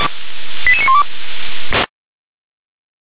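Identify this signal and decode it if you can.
Five-tone selective call (ZVEI), heard as quick runs of five tones, tones of about 70 ms. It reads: 88912
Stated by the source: ZVEI five-tone selective call received on an SDR.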